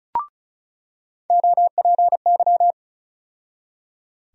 Morse code, heard as a keyed tone, about 36 wpm, 700 Hz. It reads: OPY
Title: USA Call Sign Postfixes - 3 Letters 35wpm